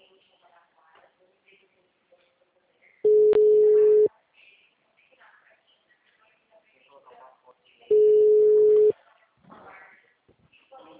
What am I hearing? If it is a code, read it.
Ringback tone (Europe, 425 Hz), with speech.